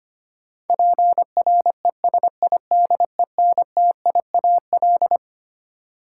Morse code, heard as a keyed tone, about 25 wpm, 700 Hz. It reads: PRESIDENTIAL
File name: Twelve Letter Words - 25wpm